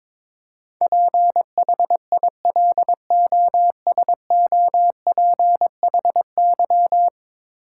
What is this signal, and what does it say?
Morse code, heard as a keyed tone, about 22 wpm, 700 Hz. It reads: PHILOSOPHY
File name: practice / 22wpm / 10 Letter Words - First 3 Letters Given - 22wpm